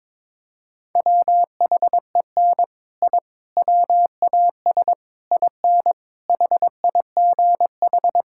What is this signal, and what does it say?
Morse code, heard as a keyed tone, about 22 wpm, 700 Hz. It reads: WHEN I WAS IN HIGH
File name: practice / 22wpm / Sets of 6 Words - 22wpm